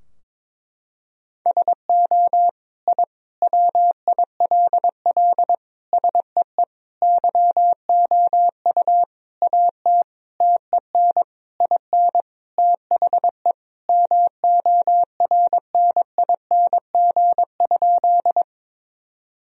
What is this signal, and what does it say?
Morse code, heard as a keyed tone, about 22 wpm, 700 Hz. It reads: SO I WILL SEE YOU AT TEN IN THE MORNING?
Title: Sentences from Top 1000 Words - Review Part 2 - 22wpm